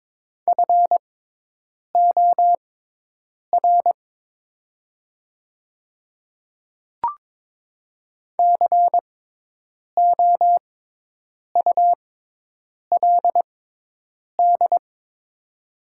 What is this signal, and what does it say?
Morse code, heard as a keyed tone, about 22 wpm, spaced as for 8 wpm, 700 Hz. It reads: FOR COULD
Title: 6x Character Spacing - Top 100 Words - 22wpm